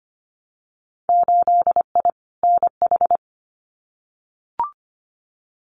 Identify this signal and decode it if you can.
Morse code, heard as a keyed tone, about 25 wpm, 700 Hz. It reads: 8I NH
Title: Winter Field Day Exchanges Encore 25wpm